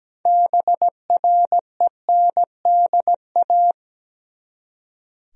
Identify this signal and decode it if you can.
Morse code, heard as a keyed tone, about 17 wpm, 700 Hz. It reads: BRENDA
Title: USA Common Names - 17wpm